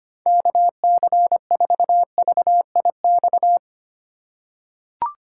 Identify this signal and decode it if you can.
Morse code, heard as a keyed tone, about 25 wpm, 700 Hz. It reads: KC4VIX
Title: USA Call Signs - 8x Character Spacing - 25wpm